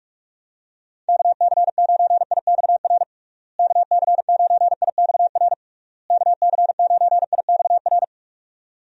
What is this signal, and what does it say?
Morse code, heard as a keyed tone, about 46 wpm, 700 Hz. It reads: KC9IXR KC9IXR KC9IXR
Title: USA Call Signs - 6 Characters - Repeated 3x - 45wpm